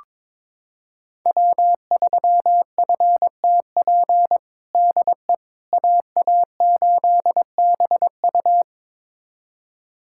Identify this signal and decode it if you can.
Morse code, heard as a keyed tone, about 22 wpm, 700 Hz. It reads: W3FTP DE AA8BU
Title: USA Callsign DE Callsign - 5 Characters - 22wpm